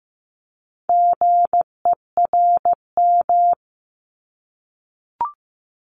Morse code, heard as a keyed tone, about 15 wpm, 700 Hz. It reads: GERM
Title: Multi-letter Phonemes Repeated 3x 15wpm